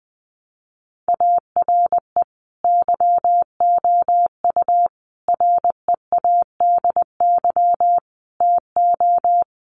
Morse code, heard as a keyed tone, about 20 wpm, 700 Hz. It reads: ARE YOU READY TO
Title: Sentences from Top 400 Words 20wpm